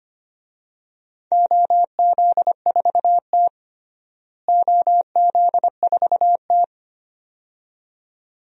Morse code, heard as a keyed tone, about 25 wpm, 700 Hz. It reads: OZ4T OZ4T
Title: DXCC-100 Call Signs - 4 Characters - Repeated 2x - 25wpm